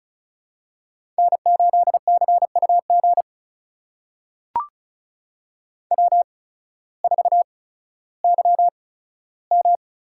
Morse code, heard as a keyed tone, about 36 wpm, spaced as for 16 wpm, 700 Hz. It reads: N8CUG W4YM
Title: USA Call Signs - 5 Characters - 8x Character Spacing - 35wpm